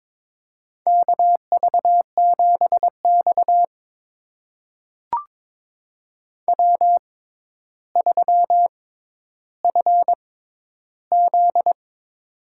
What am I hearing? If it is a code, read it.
Morse code, heard as a keyed tone, about 22 wpm, spaced as for 11 wpm, 700 Hz. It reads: KV7X W3FZ
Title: USA Call Signs - 4 Characters - 6x Character Spacing - 22wpm